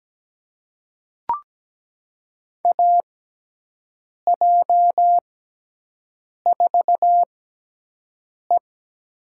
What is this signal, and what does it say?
Morse code, heard as a keyed tone, about 17 wpm, spaced as for 6 wpm, 700 Hz. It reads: AJ4E
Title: USA Call Signs - 5 Characters - 6x Character Spacing - 17wpm